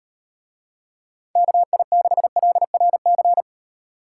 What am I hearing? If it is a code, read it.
Morse code, heard as a keyed tone, about 34 wpm, 700 Hz. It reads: KI6LRC